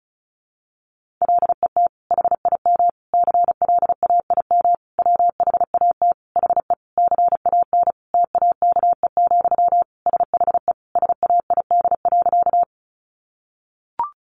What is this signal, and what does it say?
Morse code, heard as a keyed tone, about 34 wpm, 700 Hz. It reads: LET HIM CLAIM WHAT HE CAN TAKE, SHE SAID.
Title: Sentences from Top 1500 Words 35wpm